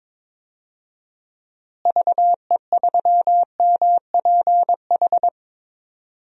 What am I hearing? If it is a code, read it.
Morse code, heard as a keyed tone, about 22 wpm, 700 Hz. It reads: VE3MPH
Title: DXCC-100 Call Signs - 6 Characters - Repeated 3x - 22wpm